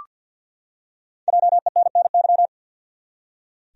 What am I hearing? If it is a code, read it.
Morse code, heard as a keyed tone, about 46 wpm, 700 Hz. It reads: JENNY
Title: USA Common Names - Max 5 Letters - 50wpm